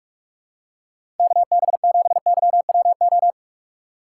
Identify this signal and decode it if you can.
Morse code, heard as a keyed tone, about 46 wpm, 700 Hz. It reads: KB7YWO
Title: USA Call Signs - 6 Characters - 4x Character Spacing - 45wpm